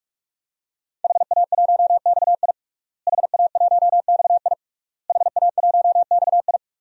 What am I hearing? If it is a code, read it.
Morse code, heard as a keyed tone, about 43 wpm, 700 Hz. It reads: HA1XI HA1XI HA1XI
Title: DXCC-100 Call Signs - 5 Characters - Repeated 3x - 45wpm